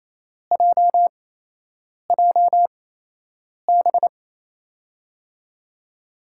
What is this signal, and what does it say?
Morse code, heard as a keyed tone, about 28 wpm, spaced as for 8 wpm, 700 Hz. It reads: JJB
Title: USA Call Signs - 8x Character Spacing - 28wpm